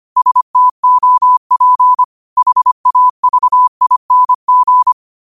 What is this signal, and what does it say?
Morse code, heard as a keyed tone, about 25 wpm, 1000 Hz. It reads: STOP HAVING
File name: qotdfiles / 25wpm